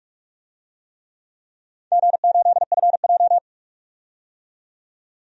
Morse code, heard as a keyed tone, about 45 wpm, 700 Hz. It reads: G8FJ
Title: DXCC-100 Call Signs - 4 Characters - 45wpm